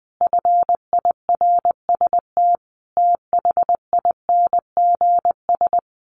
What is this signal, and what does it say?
Morse code, heard as a keyed tone, about 20 wpm, 700 Hz. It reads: FIRST THINGS
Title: Sets of 5 Words Encore 20wpm